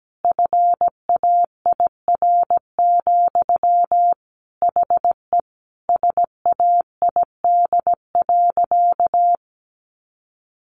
Morse code, heard as a keyed tone, about 17 wpm, 700 Hz. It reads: FAIR, HE SAID.
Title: Sentences from Top 1300 Words 17wpm